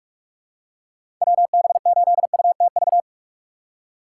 Morse code, heard as a keyed tone, about 47 wpm, 700 Hz. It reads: WB8UTV
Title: USA Call Signs - 4x Character Spacing - 45wpm